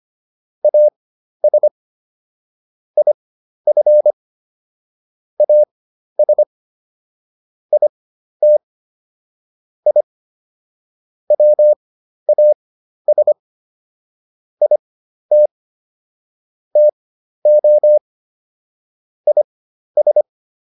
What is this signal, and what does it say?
Morse code, heard as a keyed tone, about 25 wpm, spaced as for 12 wpm, 600 Hz. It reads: AS IF AS IT I WAS IT TO IS